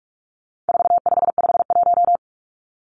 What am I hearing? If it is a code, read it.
Morse code, heard as a keyed tone, about 44 wpm, 700 Hz. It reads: 4551
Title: Numbers - 4-Digit 45wpm